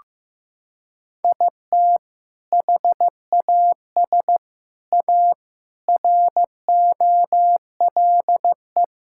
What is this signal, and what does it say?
Morse code, heard as a keyed tone, about 16 wpm, 700 Hz. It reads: IT HAS A ROLE
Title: Sentences from Top 1000 Words - Review Part 6 - 15wpm